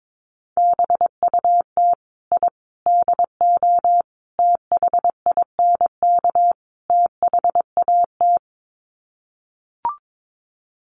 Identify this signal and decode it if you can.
Morse code, heard as a keyed tone, about 22 wpm, 700 Hz. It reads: BUT I DO THINK THAT
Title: Sets of 5 Words Encore2 22wpm